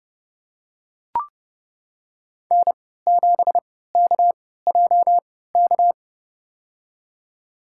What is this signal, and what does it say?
Morse code, heard as a keyed tone, about 30 wpm, spaced as for 17 wpm, 700 Hz. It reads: N7KJK
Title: USA Call Signs - 5 Characters - 3x Character Spacing - 30wpm